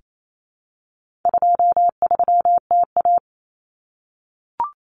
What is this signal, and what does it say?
Morse code, heard as a keyed tone, about 28 wpm, 700 Hz. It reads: 23TA
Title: Common Cut Numbers - 4-Digit 28wpm